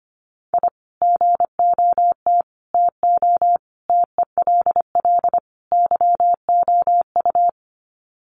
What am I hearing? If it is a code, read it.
Morse code, heard as a keyed tone, about 25 wpm, 700 Hz. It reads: I GOT TO TELL YOU
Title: Sets of 5 Words Encore2 25wpm